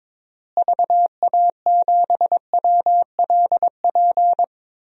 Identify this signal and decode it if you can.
Morse code, heard as a keyed tone, about 22 wpm, 700 Hz. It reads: VA7WLP
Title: DXCC-100 Call Signs - 6 Characters - Repeated 2x - 22wpm